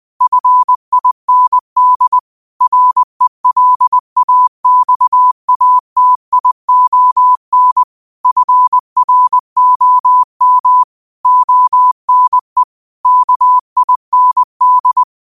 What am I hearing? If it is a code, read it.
Morse code, heard as a keyed tone, about 20 wpm, 1000 Hz. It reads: FIND RELAXATION FROM ONE KIND
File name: qotdfiles / 20wpm